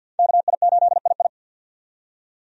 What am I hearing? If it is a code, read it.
Morse code, heard as a keyed tone, about 51 wpm, 700 Hz. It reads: KI8II
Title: USA Call Signs - 5 Characters - 50wpm